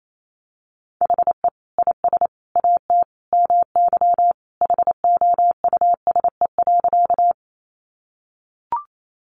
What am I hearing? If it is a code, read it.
Morse code, heard as a keyed tone, about 28 wpm, 700 Hz. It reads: HE IS AT MY HOUSE.
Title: Sentences from Top 200 Words 28wpm